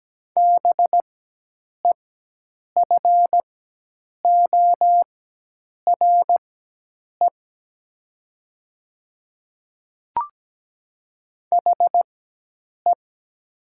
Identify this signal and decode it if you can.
Morse code, heard as a keyed tone, about 17 wpm, spaced as for 8 wpm, 700 Hz. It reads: BEFORE HE
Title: 4x Character Spacing - Top 200 Words - 17wpm